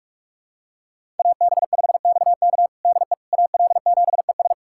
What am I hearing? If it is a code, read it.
Morse code, heard as a keyed tone, about 47 wpm, 700 Hz. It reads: AB5XK DE AL7ES
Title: USA Callsign DE Callsign - 5 Characters - 45wpm